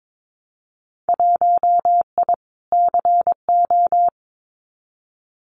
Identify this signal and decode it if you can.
Morse code, heard as a keyed tone, about 22 wpm, 700 Hz. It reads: 1I CO
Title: Winter Field Day Exchanges Encore 22wpm